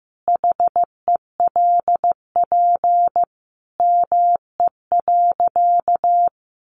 Morse code, heard as a keyed tone, about 15 wpm, 700 Hz. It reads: HELP ME.
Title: Sentences from Top 200 Words 15wpm